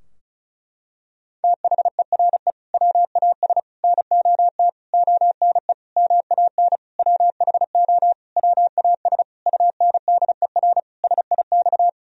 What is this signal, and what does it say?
Morse code, heard as a keyed tone, about 35 wpm, 700 Hz. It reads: THERE WAS NOT ONE MAN WHO WAS UNDER SIX
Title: Sentences from Top 1000 Words - Review Part 2 - 35wpm